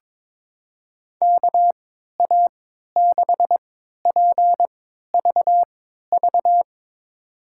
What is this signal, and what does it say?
Morse code, heard as a keyed tone, about 22 wpm, spaced as for 13 wpm, 700 Hz. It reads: KA6PVV
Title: USA Call Signs - 3x Character Spacing - 22wpm